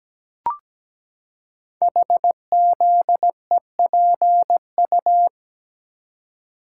Morse code, heard as a keyed tone, about 17 wpm, 700 Hz. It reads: HZEPU
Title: Code Groups - 5 Letters - 17wpm